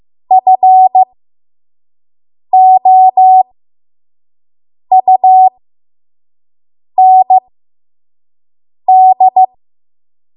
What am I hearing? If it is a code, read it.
Morse code, heard as a keyed tone, about 15 wpm, spaced as for 5 wpm, 750 Hz. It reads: FOUND
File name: Morse / 05WPM